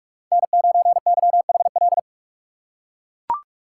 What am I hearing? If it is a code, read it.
Morse code, heard as a keyed tone, about 42 wpm, 700 Hz. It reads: N9YHL